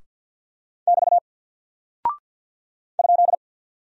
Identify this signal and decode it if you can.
Morse code, heard as a keyed tone, about 48 wpm, 700 Hz. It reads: <BT> ?
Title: Single Character (Letter:Number:Punctuation:Prosign) Alt 50wpm